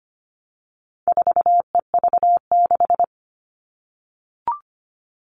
Morse code, heard as a keyed tone, about 25 wpm, 700 Hz. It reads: 4EV6
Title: All Cut Numbers - 4-Digit 25wpm